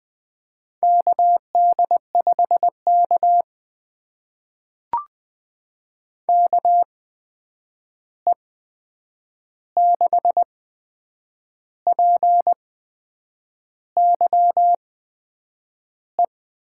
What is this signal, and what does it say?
Morse code, heard as a keyed tone, about 20 wpm, spaced as for 8 wpm, 700 Hz. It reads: KD5K KE6PYE
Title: USA Call Signs - 8x Character Spacing - 20wpm